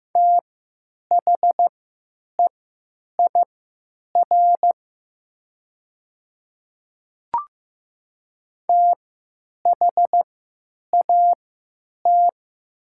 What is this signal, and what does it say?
Morse code, heard as a keyed tone, about 15 wpm, spaced as for 9 wpm, 700 Hz. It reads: THEIR THAT
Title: 3x Character Spacing - Top 100 Words - 15wpm